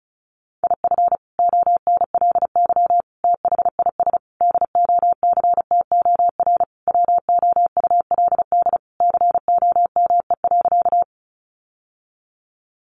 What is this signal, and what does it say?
Morse code, heard as a keyed tone, about 34 wpm, 700 Hz. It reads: IF ONLY THIS DOCTOR WOULD COME.
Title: Sentences from Top 1500 Words 35wpm